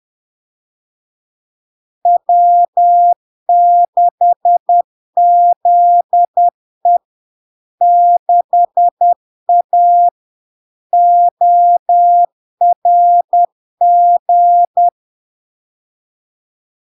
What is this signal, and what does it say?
Morse code, heard as a keyed tone, about 10 wpm, 700 Hz. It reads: W6ZE 6A ORG
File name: LARK-FD-2022-Log-in-CW-10wpm-700-Hz